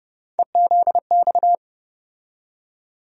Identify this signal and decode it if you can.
Morse code, heard as a keyed tone, about 30 wpm, 700 Hz. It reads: EZX
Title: USA Call Sign Postfixes - 3 Letters 30wpm